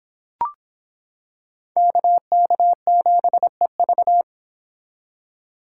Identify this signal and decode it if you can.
Morse code, heard as a keyed tone, about 26 wpm, 700 Hz. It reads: KK7EV